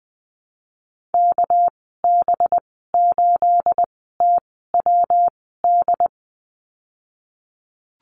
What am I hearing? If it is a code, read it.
Morse code, heard as a keyed tone, about 20 wpm, spaced as for 14 wpm, 700 Hz. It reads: KB8TWD
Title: 219 Lesson 35g - Number 0 - Call Signs 20wpm